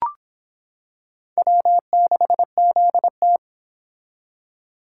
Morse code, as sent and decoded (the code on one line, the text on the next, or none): .-- -.... --.. -
W6ZT